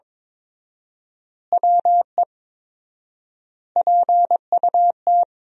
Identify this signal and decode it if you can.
Morse code, heard as a keyed tone, about 22 wpm, 700 Hz. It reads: WE PUT
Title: 4x Word-Spacing - Sentences from Top 200 Words - 22wpm